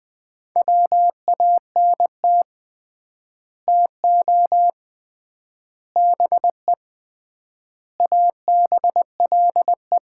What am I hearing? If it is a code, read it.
Morse code, heard as a keyed tone, about 20 wpm, 700 Hz. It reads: WANT TO BE ABLE
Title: 3x Word-Spacing - Sets of 5 Words - 20wpm